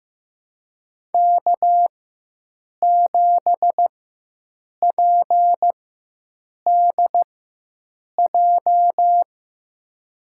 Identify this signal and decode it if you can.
Morse code, heard as a keyed tone, about 15 wpm, spaced as for 7 wpm, 700 Hz. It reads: K7PDJ
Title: USA Call Signs - 5 Characters - 4x Character Spacing - 15wpm